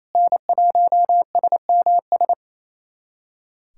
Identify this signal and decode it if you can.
Morse code, heard as a keyed tone, about 28 wpm, 700 Hz. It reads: N1SMS
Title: USA Callsign DE Callsign - 5 Characters - 28wpm